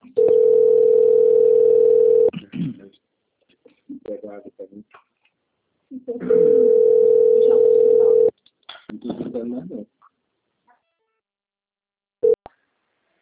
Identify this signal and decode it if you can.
Ringback tone (North America), 440 + 480 Hz, with speech.
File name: out- - - - - .88633